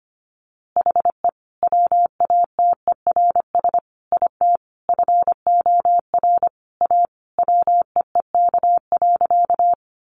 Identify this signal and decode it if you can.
Morse code, heard as a keyed tone, about 25 wpm, 700 Hz. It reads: HE WATERS IT FOR A WEEK.
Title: Easy Phrases 25wpm